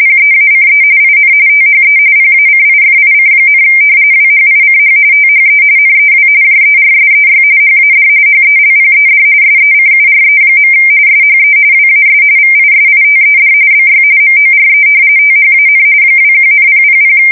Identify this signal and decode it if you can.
RTTY, about 45.45 baud. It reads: PEACE IN THE PACIFIC, WHICH THE NEW JERSEY HAD HELPED SECURE. SHE WAS VISITED BY MORE THAN 21,000
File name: fsk170 / battleshipnj